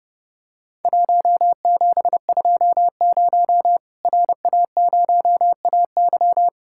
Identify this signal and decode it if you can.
Morse code, heard as a keyed tone, about 30 wpm, 700 Hz. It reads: 1720 RA0AY
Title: Worked All Europe DX Contest - QTC Exchange - 30wpm